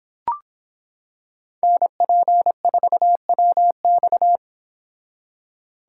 Morse code, heard as a keyed tone, about 25 wpm, 700 Hz. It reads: NP4WX